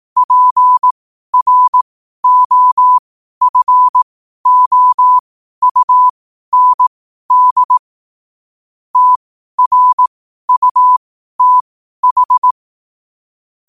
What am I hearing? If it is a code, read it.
Morse code, heard as a keyed tone, about 18 wpm, spaced as for 12 wpm, 1000 Hz. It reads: PROFOUND TRUTH
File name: qotdfiles / 12wpm